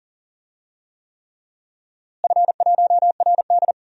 Morse code, heard as a keyed tone, about 40 wpm, 700 Hz. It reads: F1RD
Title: DXCC-100 Call Signs - 4 Characters - Repeated 3x - 40wpm